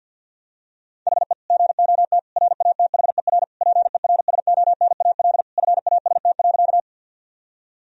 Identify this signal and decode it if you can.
Morse code, heard as a keyed tone, about 49 wpm, 700 Hz. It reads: HE GOT RATHER PERSONAL FAST.
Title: Sentences from Top 1000 Words - Review Part 4 - 50wpm